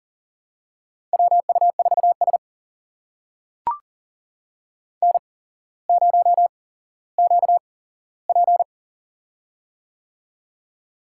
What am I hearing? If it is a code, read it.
Morse code, heard as a keyed tone, about 38 wpm, spaced as for 17 wpm, 700 Hz. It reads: WU4S N0QP
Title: USA Call Signs - 4 Characters - 8x Character Spacing - 40wpm